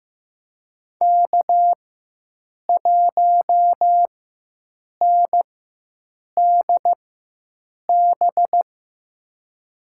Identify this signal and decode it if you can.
Morse code, heard as a keyed tone, about 15 wpm, spaced as for 7 wpm, 700 Hz. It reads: K1NDB
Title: USA Call Signs - 5 Characters - 4x Character Spacing - 15wpm